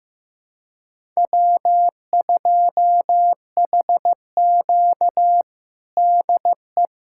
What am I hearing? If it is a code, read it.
Morse code, heard as a keyed tone, about 15 wpm, 700 Hz. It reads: W2HQ DE
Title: USA Callsign DE Callsign - 4 Characters - 15wpm